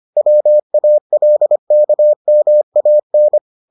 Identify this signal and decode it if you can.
Morse code, heard as a keyed tone, about 25 wpm, 600 Hz. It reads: WALKMAN